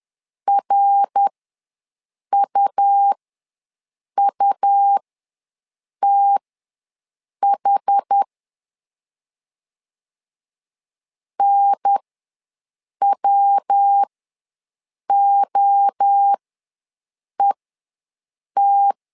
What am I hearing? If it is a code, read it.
Morse code, heard as a keyed tone, about 10 wpm, spaced as for 6 wpm, 800 Hz. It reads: RUUTH NWOET
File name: LEZIONI / 3 MWRO / 3 MWRO ws 10wpm 2310